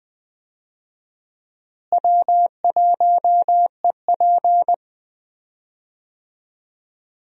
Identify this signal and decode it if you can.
Morse code, heard as a keyed tone, about 20 wpm, 700 Hz. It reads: W1EP
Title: USA Call Signs - 4 Characters - Repeated 3x - 20wpm